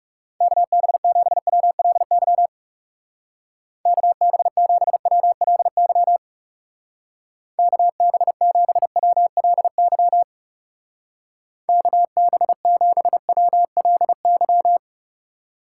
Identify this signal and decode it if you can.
Morse code, heard as a keyed tone, about 39 wpm, 700 Hz. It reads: KB7WLY KB7WLY KB7WLY KB7WLY